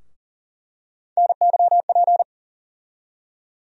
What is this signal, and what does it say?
Morse code, heard as a keyed tone, about 40 wpm, 700 Hz. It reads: NYP
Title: USA Call Sign Postfixes - 3 Letters 40wpm